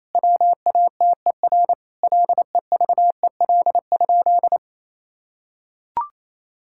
Morse code, heard as a keyed tone, about 28 wpm, 700 Hz. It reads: WATER LEVEL?
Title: Sentences from Top 1000 Words - Review Part 5 - 28wpm